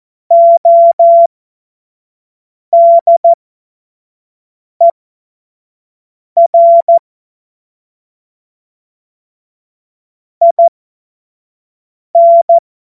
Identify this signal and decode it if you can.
Morse code, heard as a keyed tone, about 14 wpm, spaced as for 5 wpm, 680 Hz. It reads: ODER IN
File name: Morsekurs / Witze / Witz 12. Oktober 2021, 25 F70